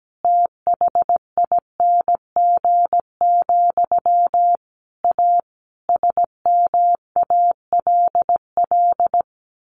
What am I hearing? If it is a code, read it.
Morse code, heard as a keyed tone, about 17 wpm, 700 Hz. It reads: THING, A SMALL
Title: A Sound Of Thunder 17wpm